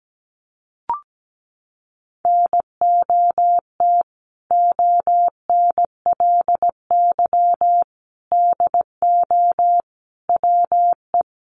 Morse code, heard as a keyed tone, about 17 wpm, 700 Hz. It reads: NOT ONLY DO WE
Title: Sentences from Top 1200 Words 17wpm